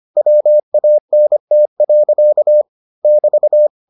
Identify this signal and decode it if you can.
Morse code, heard as a keyed tone, about 25 wpm, 600 Hz. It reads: WANT. <BT>